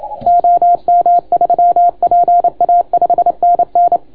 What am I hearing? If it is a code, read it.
Morse code, heard as a keyed tone, about 27 wpm, 690 Hz. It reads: OM3PA 5NN